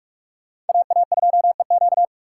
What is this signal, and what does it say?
Morse code, heard as a keyed tone, about 56 wpm, 700 Hz. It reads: AA2EQ
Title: Speed Racing - USA 2x2 Call Signs 45wpm